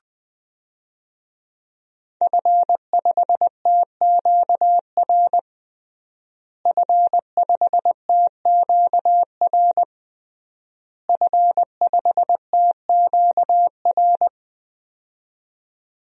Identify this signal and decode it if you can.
Morse code, heard as a keyed tone, about 21 wpm, 700 Hz. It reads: F5TQR F5TQR F5TQR
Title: DXCC-100 Call Signs - 5 Characters - Repeated 3x - 20wpm